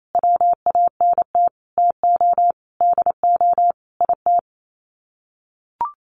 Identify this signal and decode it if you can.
Morse code, heard as a keyed tone, about 28 wpm, 700 Hz. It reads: WANT TO DO IT
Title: Sets of 5 Words Encore 28wpm